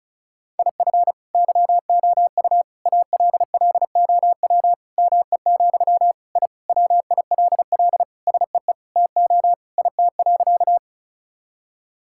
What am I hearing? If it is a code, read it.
Morse code, heard as a keyed tone, about 36 wpm, 700 Hz. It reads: IF YOU ALLOW ME, I WILL SEE TO IT.
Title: Sentences from Top 1000 Words - Review Part 5 - 35wpm